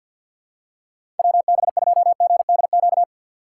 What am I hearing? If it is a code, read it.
Morse code, heard as a keyed tone, about 49 wpm, 700 Hz. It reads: WB2GDQ
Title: USA Call Signs - 6 Characters - 3x Character Spacing - 50wpm